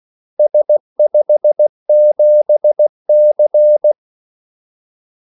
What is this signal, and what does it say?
Morse code, heard as a keyed tone, about 16 wpm, 600 Hz. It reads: S57C
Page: Kaldesignaler 16 wpm